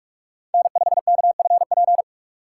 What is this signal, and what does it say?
Morse code, heard as a keyed tone, about 44 wpm, 700 Hz. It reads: N5KFP
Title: USA Call Signs - 5 Characters - 6x Character Spacing - 45wpm